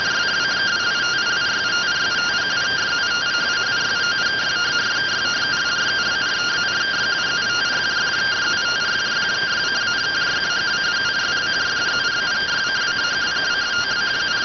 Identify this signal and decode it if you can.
Two-tone teleprinter signal, about 170 Hz shift.